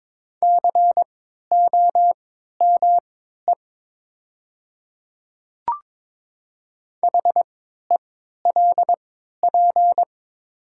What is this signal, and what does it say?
Morse code, heard as a keyed tone, about 22 wpm, spaced as for 12 wpm, 700 Hz. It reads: COME HELP
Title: 3x Character Spacing - Top 300 Words - 22wpm